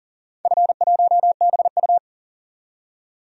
Morse code, heard as a keyed tone, about 40 wpm, 700 Hz. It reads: F1BU
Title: DXCC-100 Call Signs - 4 Characters - 40wpm